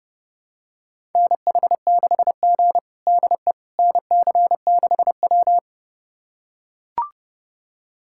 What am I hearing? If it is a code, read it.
Morse code, heard as a keyed tone, about 30 wpm, 700 Hz. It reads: NH6G DE NC6W
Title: USA Callsign DE Callsign - 4 Characters - 30wpm